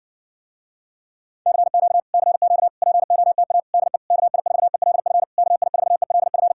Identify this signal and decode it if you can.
Morse code, heard as a keyed tone, about 60 wpm, 700 Hz. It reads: CQ CQ POTA DE KI4ELF KI4ELF